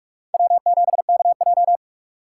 Morse code, heard as a keyed tone, about 41 wpm, 700 Hz. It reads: W7KJ
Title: USA Call Signs - 4 Characters - 8x Character Spacing - 45wpm